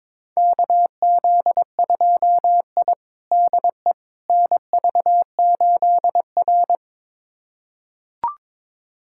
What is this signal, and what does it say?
Morse code, heard as a keyed tone, about 21 wpm, 700 Hz. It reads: KZ2I DE NV8R
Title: USA Callsign DE Callsign - 4 Characters - 22wpm